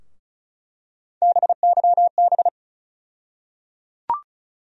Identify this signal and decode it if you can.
Morse code, heard as a keyed tone, about 34 wpm, 700 Hz. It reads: BYB
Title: USA Call Sign Postfixes - 3 Letters 35wpm